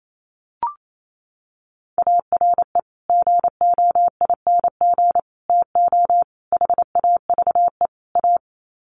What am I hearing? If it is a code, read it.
Morse code, heard as a keyed tone, about 28 wpm, 700 Hz. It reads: ARE GOING TO HAVE A
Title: Sets of 5 Words Encore2 28wpm